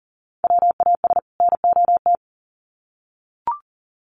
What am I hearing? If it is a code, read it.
Morse code, heard as a keyed tone, about 40 wpm, 700 Hz. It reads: WAS NOT
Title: Sets of 2 Words 40wpm